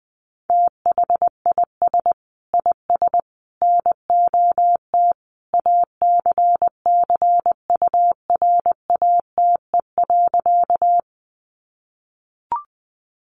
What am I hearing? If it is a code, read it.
Morse code, heard as a keyed tone, about 20 wpm, 700 Hz. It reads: THIS IS NOT ACCURATE.
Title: Sentences from Top 1200 Words 20wpm